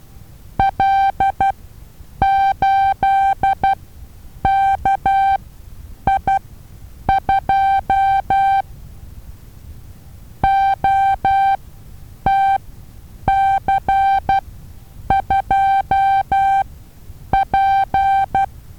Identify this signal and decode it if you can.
Morse code, heard as a keyed tone, about 12 wpm, spaced as for 8 wpm, 780 Hz. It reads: L8KI2 OTC2P
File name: Taviro HA6VA B